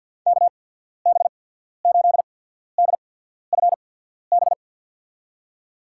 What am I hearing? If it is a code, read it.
Morse code, heard as a keyed tone, about 51 wpm, spaced as for 14 wpm, 700 Hz. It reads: KB8DFB
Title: USA Call Signs - 6 Characters - 8x Character Spacing - 50wpm